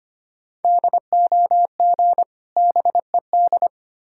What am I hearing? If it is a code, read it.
Morse code, heard as a keyed tone, about 25 wpm, 700 Hz. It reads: DOG BED
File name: practice / 25wpm / Difficult Words with Hints - 25wpm